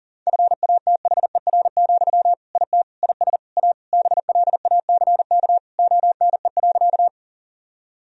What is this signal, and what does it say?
Morse code, heard as a keyed tone, about 40 wpm, 700 Hz. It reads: FATHER, IT IS A BLACK ONE.
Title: Sentences from Top 1000 Words - Review Part 2 - 40wpm